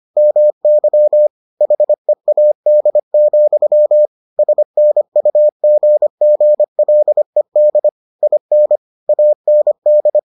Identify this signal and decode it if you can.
Morse code, heard as a keyed tone, about 25 wpm, 600 Hz. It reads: MY HEAD, SNUGGLED IN AND